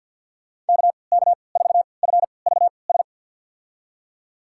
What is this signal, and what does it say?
Morse code, heard as a keyed tone, about 52 wpm, spaced as for 29 wpm, 700 Hz. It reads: KK4FVS
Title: USA Call Signs - 3x Character Spacing - 50wpm